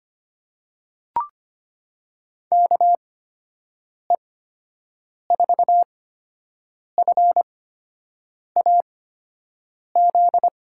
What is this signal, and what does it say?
Morse code, heard as a keyed tone, about 25 wpm, spaced as for 7 wpm, 700 Hz. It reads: KE4FAZ
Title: USA Call Signs - 8x Character Spacing - 25wpm